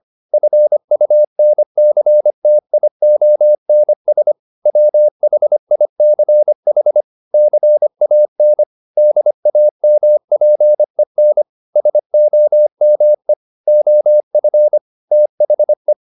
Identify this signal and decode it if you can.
Morse code, heard as a keyed tone, about 25 wpm, 600 Hz. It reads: FUNCTIONS WHICH CAN DAMPEN SOME OF THE